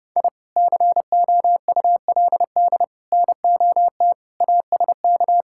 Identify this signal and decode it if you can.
Morse code, heard as a keyed tone, about 30 wpm, 700 Hz. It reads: I COULD NOT ASK
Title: Sentences from Top 1000 Words - Review Part 5 - 30wpm